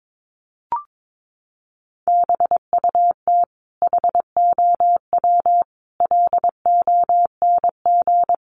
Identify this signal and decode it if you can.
Morse code, heard as a keyed tone, about 22 wpm, 700 Hz. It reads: BUT HOW LONG
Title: Sentences from Top 500 Words 22wpm